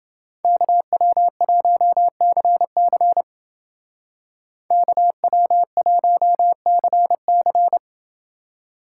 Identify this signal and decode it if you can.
Morse code, heard as a keyed tone, about 29 wpm, 700 Hz. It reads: KW1CC KW1CC